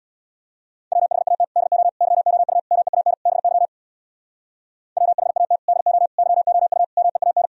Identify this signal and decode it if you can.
Morse code, heard as a keyed tone, about 64 wpm, 700 Hz. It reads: WHAT DO YOU MEAN BY WHAT DO YOU MEAN